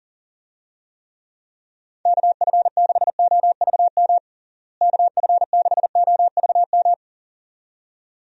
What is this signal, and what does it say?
Morse code, heard as a keyed tone, about 40 wpm, 700 Hz. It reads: KF6OVM KF6OVM
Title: USA Call Signs - 6 Characters - Repeated 2x - 40wpm